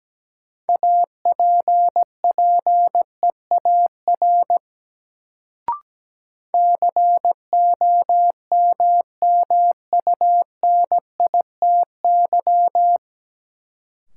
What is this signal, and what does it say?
Morse code, heard as a keyed tone, about 17 wpm, 700 Hz. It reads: APPEAR COMMUNITY
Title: Sentences from Top 1200 Words - New Word Drill Alt - 17wpm